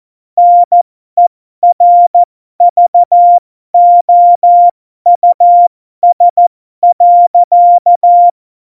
Morse code, heard as a keyed tone, about 14 wpm, spaced as for 12 wpm, 700 Hz. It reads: NERVOUS.
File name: Morsekurs / Witze